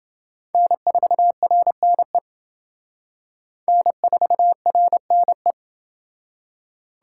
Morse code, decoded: N4RNE N4RNE

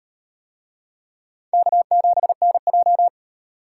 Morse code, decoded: K7NJ

